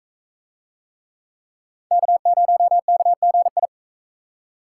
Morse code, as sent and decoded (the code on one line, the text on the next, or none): -.- ----- -.- --. ..
K0KGI